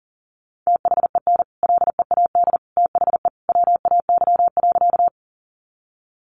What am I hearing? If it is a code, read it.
Morse code, heard as a keyed tone, about 40 wpm, 700 Hz. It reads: THEN LEAD THE WAY.
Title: Sentences from Top 500 Words 40wpm